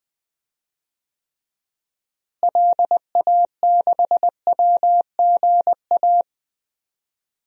Morse code, heard as a keyed tone, about 20 wpm, 700 Hz. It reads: LA6WGA
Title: DXCC-100 Call Signs - 6 Characters - Repeated 2x - 20wpm